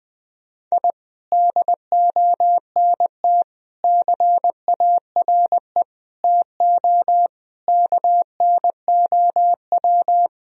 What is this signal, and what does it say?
Morse code, heard as a keyed tone, about 20 wpm, 700 Hz. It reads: I DONT CARE TO KNOW
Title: Sentences from Top 1000 Words - Review Part 1 - 20wpm